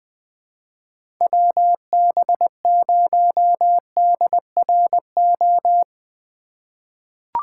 Morse code, heard as a keyed tone, about 20 wpm, 700 Hz. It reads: WB0DRO
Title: USA Call Signs - 6 Characters - 2x Character Spacing - 20wpm